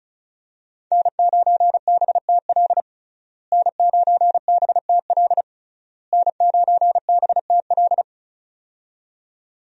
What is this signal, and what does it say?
Morse code, heard as a keyed tone, about 34 wpm, 700 Hz. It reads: N9BTL N9BTL N9BTL
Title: USA Call Signs - 5 Characters - Repeated 3x - 35wpm